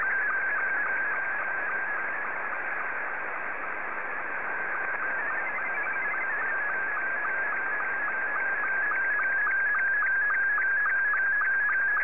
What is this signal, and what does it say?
SSTV (slow-scan television), Scottie 2.